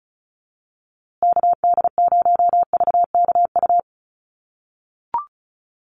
Morse code, heard as a keyed tone, about 34 wpm, 700 Hz. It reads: KD0VKU
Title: USA Call Signs Repeated 2x 35wpm